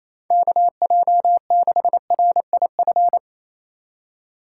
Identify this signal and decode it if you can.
Morse code, heard as a keyed tone, about 28 wpm, 700 Hz. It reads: KJ6RIF
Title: USA Call Signs - 6 Characters - 2x Character Spacing - 28wpm